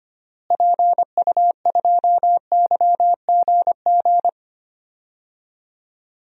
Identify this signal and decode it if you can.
Morse code, heard as a keyed tone, about 25 wpm, 700 Hz. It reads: PU2YGG
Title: DXCC-100 Call Signs - 6 Characters - Repeated 3x - 25wpm